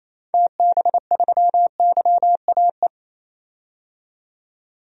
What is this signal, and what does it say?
Morse code, heard as a keyed tone, about 28 wpm, 700 Hz. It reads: TB3YAE
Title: DXCC-100 Call Signs - 6 Characters - 28wpm